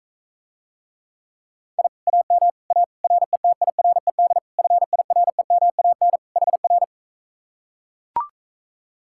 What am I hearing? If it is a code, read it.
Morse code, heard as a keyed tone, about 44 wpm, 700 Hz. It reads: I AM A RETIRED FIREMAN HR